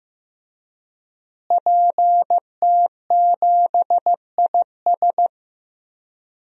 Morse code, heard as a keyed tone, about 15 wpm, 700 Hz. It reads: PT7IS
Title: DXCC-100 Call Signs - 5 Characters - Repeated 2x - 15wpm